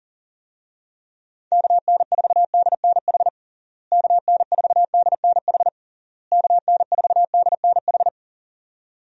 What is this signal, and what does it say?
Morse code, heard as a keyed tone, about 40 wpm, 700 Hz. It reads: KN4DNH KN4DNH KN4DNH
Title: USA Call Signs - 6 Characters - Repeated 3x - 40wpm